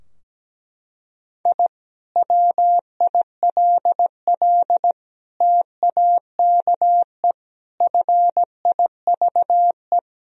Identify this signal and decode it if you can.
Morse code, heard as a keyed tone, about 17 wpm, 700 Hz. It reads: I WILL TAKE FIVE
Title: Sentences from Top 1000 Words - Review Part 2 - 17wpm